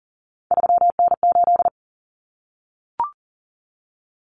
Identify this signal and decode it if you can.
Morse code, heard as a keyed tone, about 40 wpm, 700 Hz. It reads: 3N8
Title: Common Cut Numbers - 3-Digit 40wpm